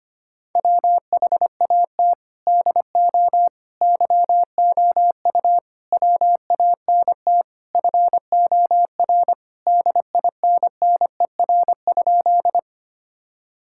Morse code, heard as a keyed tone, about 25 wpm, 700 Hz. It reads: WHAT DO YOU WANT FOR DINNER?
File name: practice / 25wpm / Sentences with 4-Word Phrases - 25wpm